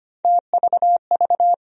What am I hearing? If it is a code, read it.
Morse code, heard as a keyed tone, about 25 wpm, 700 Hz. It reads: TVV
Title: USA Call Sign Postfixes - 3 Letters 25wpm